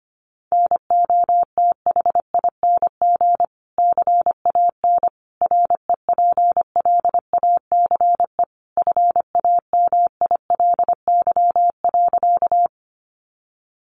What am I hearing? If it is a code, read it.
Morse code, heard as a keyed tone, about 25 wpm, 700 Hz. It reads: NOTHING CAN REPLACE FAMILY.
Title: Sentences from Top 1400 Words 25wpm